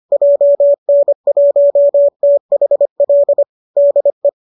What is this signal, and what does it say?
Morse code, heard as a keyed tone, about 25 wpm, 590 Hz. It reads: JN1THL DE
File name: qso205 25